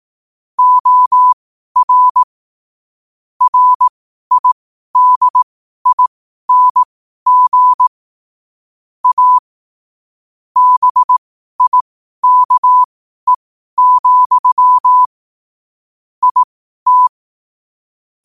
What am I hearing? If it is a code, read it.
Morse code, heard as a keyed tone, about 18 wpm, spaced as for 12 wpm, 1000 Hz. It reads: OR RIDING A BIKE, IT
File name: qotdfiles / 12wpm